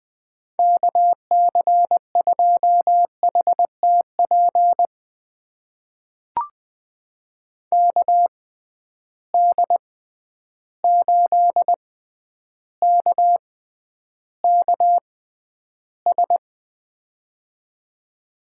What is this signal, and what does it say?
Morse code, heard as a keyed tone, about 20 wpm, spaced as for 10 wpm, 700 Hz. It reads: KC2HTP KD8KKS
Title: USA Call Signs - 6 Characters - 6x Character Spacing - 20wpm